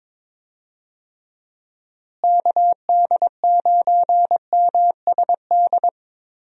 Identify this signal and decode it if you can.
Morse code, heard as a keyed tone, about 22 wpm, 700 Hz. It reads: KD9MSD
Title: USA Call Signs - 6 Characters - Repeated 2x - 22wpm